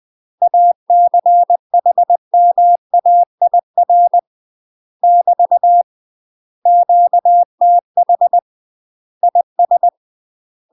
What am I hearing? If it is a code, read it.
Morse code, heard as a keyed tone, about 20 wpm, 700 Hz. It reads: ACHMAIR <BT> QTH IS
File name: UK MASTER QSOs TRAINING / 20WPM / UK MASTER QSOs 0047